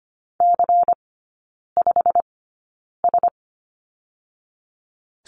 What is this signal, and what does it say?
Morse code, heard as a keyed tone, about 25 wpm, spaced as for 9 wpm, 700 Hz. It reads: C5S
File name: course / 25:10wpm / 100 Lesson 17e - Letter u - Three Characters 25wpm